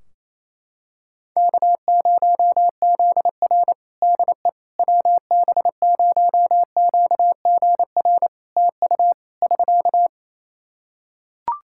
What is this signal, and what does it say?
Morse code, heard as a keyed tone, about 28 wpm, 700 Hz. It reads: K0ZR DE WB0QGR TU <SK>
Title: Practice QSO - Exchange 4 - Ending - 28wpm